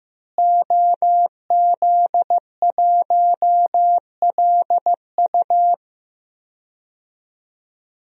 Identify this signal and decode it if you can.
Morse code, heard as a keyed tone, about 15 wpm, 700 Hz. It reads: OZ1LU
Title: DXCC-100 Call Signs - 5 Characters - Repeated 2x - 15wpm